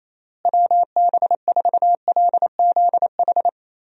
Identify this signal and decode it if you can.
Morse code, heard as a keyed tone, about 27 wpm, 700 Hz. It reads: WB4LZH
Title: USA Call Signs - 6 Characters - Repeated 3x - 28wpm